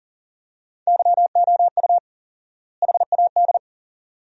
Morse code, heard as a keyed tone, about 37 wpm, 700 Hz. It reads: YOU HAD
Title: 4x Word-Spacing - Sentences from Top 400 Words - 40wpm